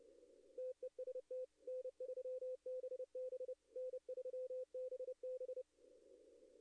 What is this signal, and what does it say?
Morse code, heard as a keyed tone, about 30 wpm, 510 Hz. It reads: TEST N3BB N3BB
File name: contests / arrl-160-k5tr-op-w5tn -2300